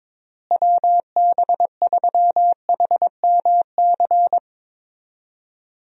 Morse code, heard as a keyed tone, about 21 wpm, 700 Hz. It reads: WB3HMC
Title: USA Call Signs - 6 Characters - Repeated 2x - 22wpm